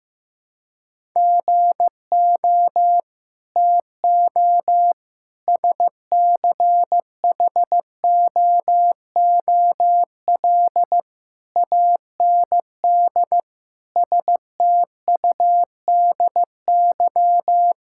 Morse code, heard as a keyed tone, about 15 wpm, 700 Hz. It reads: GO TO SCHOOL AND STUDY
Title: Sentences from Top 1000 Words - Review Part 6 - 15wpm